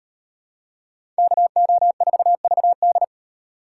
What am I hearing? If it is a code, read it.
Morse code, heard as a keyed tone, about 38 wpm, 700 Hz. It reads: KO4VD